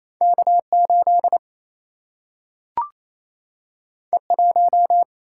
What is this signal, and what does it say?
Morse code, heard as a keyed tone, about 28 wpm, 700 Hz. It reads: K 8 E 1
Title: Single-Letter Single-Number 28wpm